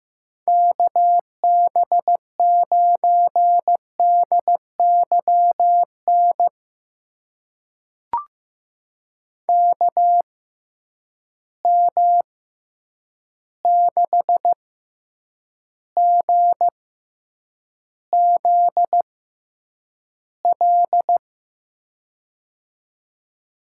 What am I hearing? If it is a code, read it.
Morse code, heard as a keyed tone, about 15 wpm, spaced as for 8 wpm, 700 Hz. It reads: KB9DYN KM6GZL
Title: USA Call Signs - 6 Characters - 6x Character Spacing - 15wpm